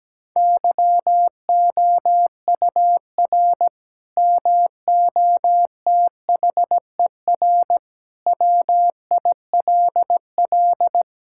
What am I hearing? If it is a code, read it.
Morse code, heard as a keyed tone, about 17 wpm, 700 Hz. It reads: YOUR MOTHER WILL
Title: Sentences from Top 1600 Words 17wpm